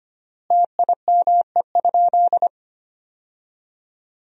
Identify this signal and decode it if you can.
Morse code, heard as a keyed tone, about 25 wpm, 700 Hz. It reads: TIME?
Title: 3x Word-Spacing - Sentences from Top 100 Words - 25wpm